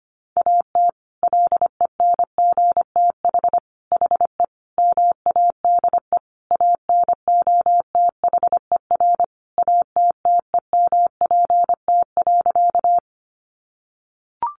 Morse code, heard as a keyed tone, about 25 wpm, 700 Hz. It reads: AT LENGTH HE MADE ANOTHER ATTEMPT.
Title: Sentences from Top 1000 Words 25wpm